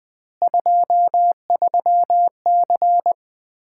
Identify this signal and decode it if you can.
Morse code, heard as a keyed tone, about 20 wpm, 700 Hz. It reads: 23C
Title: Common QSO Phrases - 2x Word-Spacing - 20wpm